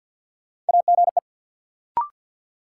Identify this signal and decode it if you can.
Morse code, heard as a keyed tone, about 46 wpm, 700 Hz. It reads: AGE
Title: Sets of 3 Words - New Word Drill - 50wpm